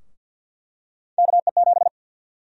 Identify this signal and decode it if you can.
Morse code, heard as a keyed tone, about 50 wpm, 700 Hz. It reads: KE7
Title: USA Call Sign Prefixes - Repeated 2x - 50wpm